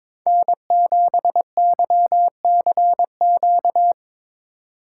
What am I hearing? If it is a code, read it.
Morse code, heard as a keyed tone, about 25 wpm, 700 Hz. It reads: N7YCQ
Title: Speed Racing - USA 1x3 Call Signs 22wpm